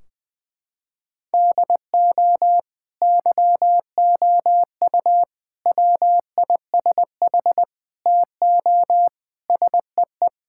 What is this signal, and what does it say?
Morse code, heard as a keyed tone, about 21 wpm, 700 Hz. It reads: DO YOU WISH TO SEE
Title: Sentences from Top 1600 Words 20wpm